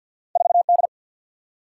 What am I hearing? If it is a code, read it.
Morse code, heard as a keyed tone, about 52 wpm, 700 Hz. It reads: 4D
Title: Single-Letter Single-Number 50wpm